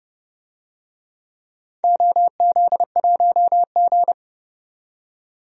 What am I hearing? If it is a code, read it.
Morse code, heard as a keyed tone, about 30 wpm, 700 Hz. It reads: OZ1G
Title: DXCC-100 Call Signs - 4 Characters - 30wpm